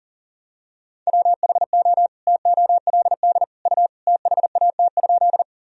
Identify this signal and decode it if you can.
Morse code, heard as a keyed tone, about 40 wpm, 700 Hz. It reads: WHO TOLD U THAT?
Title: Sentences from Top 400 Words and Abbreviations 40wpm